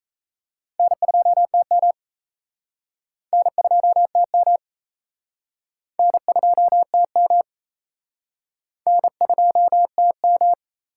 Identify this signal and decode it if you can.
Morse code, heard as a keyed tone, about 38 wpm, 700 Hz. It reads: N2TM N2TM N2TM N2TM